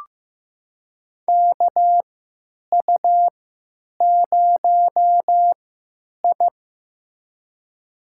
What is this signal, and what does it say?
Morse code, heard as a keyed tone, about 15 wpm, spaced as for 8 wpm, 700 Hz. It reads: KU0I
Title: USA Call Signs - 3x Character Spacing - 15wpm